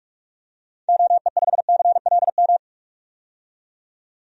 Morse code, heard as a keyed tone, about 47 wpm, 700 Hz. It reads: OE5CLM
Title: DXCC-100 Call Signs - 6 Characters - 45wpm